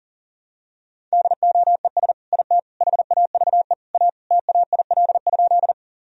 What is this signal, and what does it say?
Morse code, heard as a keyed tone, about 40 wpm, 700 Hz. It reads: DOES IT HAVE A TAIL?
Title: Sentences from Top 1000 Words - Review Part 3 - 40wpm